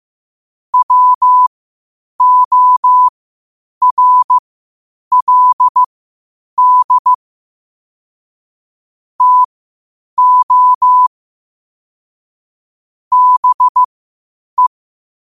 Morse code, heard as a keyed tone, about 15 wpm, spaced as for 8 wpm, 1000 Hz. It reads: WORLD TO BE
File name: qotdfiles / 8wpm